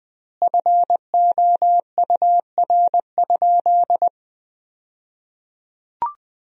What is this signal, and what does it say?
Morse code, heard as a keyed tone, about 21 wpm, 700 Hz. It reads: FOUR?
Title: 2x Word-Spacing - Sentences from Top 300 Words - 20wpm